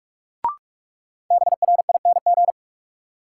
Morse code, heard as a keyed tone, about 47 wpm, 700 Hz. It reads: BRING
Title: Sentences from Top 500 Words - New Word Drill - 45wpm